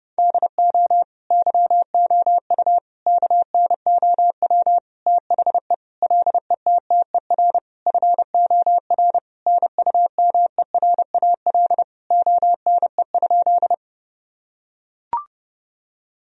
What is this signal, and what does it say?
Morse code, heard as a keyed tone, about 32 wpm, 700 Hz. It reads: DO YOU KNOW THE LETTER FOR NUMERAL ONE?
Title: Sentences from Top 1000 Words - Review Part 2 - 30wpm